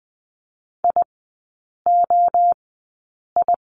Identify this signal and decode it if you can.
Morse code, heard as a keyed tone, about 20 wpm, spaced as for 8 wpm, 700 Hz. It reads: IOI
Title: 191 Lesson 31e - Letter v - Three Characters 20wpm